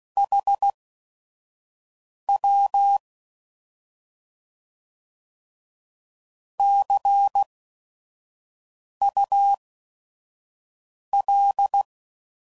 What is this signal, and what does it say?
Morse code, heard as a keyed tone, about 16 wpm, spaced as for 5 wpm, 800 Hz. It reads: HW CUL
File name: Mixed / Misc / Words